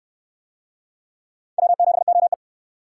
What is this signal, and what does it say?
Morse code, heard as a keyed tone, about 61 wpm, 700 Hz. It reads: W7OE